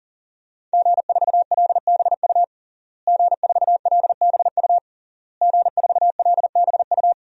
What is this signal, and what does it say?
Morse code, heard as a keyed tone, about 40 wpm, 700 Hz. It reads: G4LBU G4LBU G4LBU
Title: DXCC-100 Call Signs - 5 Characters - Repeated 3x - 40wpm